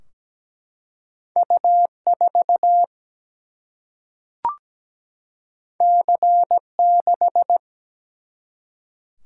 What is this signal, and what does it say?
Morse code, heard as a keyed tone, about 17 wpm, 700 Hz. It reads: U 4 C 6
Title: Single-Letter Single-Number 17wpm